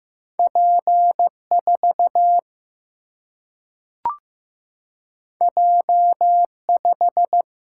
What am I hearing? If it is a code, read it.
Morse code, heard as a keyed tone, about 15 wpm, 700 Hz. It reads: P 4 J 5
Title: Single-Letter Single-Number 15wpm